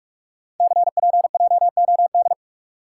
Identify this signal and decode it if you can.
Morse code, heard as a keyed tone, about 44 wpm, 690 Hz. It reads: CPJOD
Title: Code Groups - 5 Letters - 45wpm